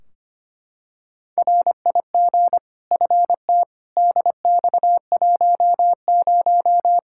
Morse code, heard as a keyed tone, about 25 wpm, 700 Hz. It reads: RIG FT DX10
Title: Practice QSO - Exchange 2 - Info - 25wpm